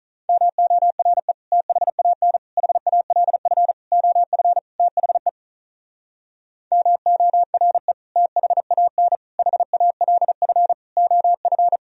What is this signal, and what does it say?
Morse code, heard as a keyed tone, about 43 wpm, 700 Hz. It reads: MORE THAN HALF OF THE MORE THAN HALF OF
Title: Speed-Racing - Sets of 5 Words 35wpm